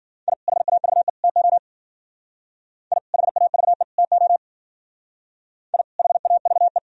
Morse code, heard as a keyed tone, about 51 wpm, 700 Hz. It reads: I HAVE TO I HAVE TO I HAVE